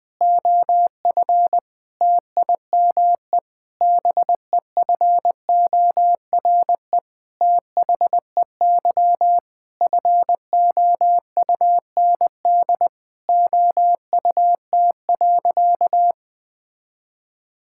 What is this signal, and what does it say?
Morse code, heard as a keyed tone, about 20 wpm, 700 Hz. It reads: OF TIME BEFORE THEY FOUND OUT.
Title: Sentences with 8-Word Phrases - 20wpm